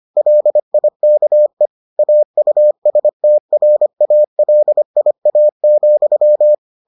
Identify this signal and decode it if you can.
Morse code, heard as a keyed tone, about 25 wpm, 600 Hz. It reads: LIKE AUSTRALIA,